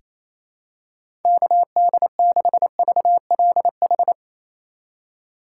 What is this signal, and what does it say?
Morse code, heard as a keyed tone, about 28 wpm, 700 Hz. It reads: KD6VLH
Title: USA Call Signs - 6 Characters - 4x Character Spacing - 28wpm